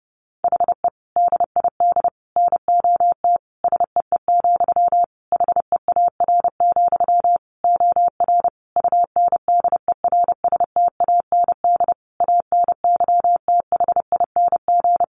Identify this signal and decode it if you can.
Morse code, heard as a keyed tone, about 30 wpm, 700 Hz. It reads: HE DID NOT SEE, HEAR, OR UNDERSTAND ANYTHING
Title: Sentences from Top 1300 Words 30wpm